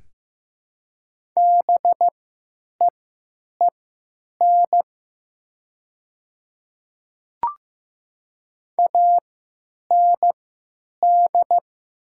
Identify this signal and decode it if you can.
Morse code, heard as a keyed tone, about 15 wpm, spaced as for 9 wpm, 700 Hz. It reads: BEEN AND
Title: 3x Character Spacing - Top 100 Words - 15wpm